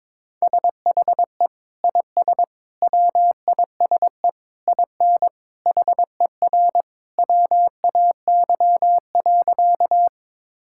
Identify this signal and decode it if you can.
Morse code, heard as a keyed tone, about 22 wpm, 700 Hz. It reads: SHE IS WISE IN HER WAY.
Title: Sentences from Top 1600 Words 22wpm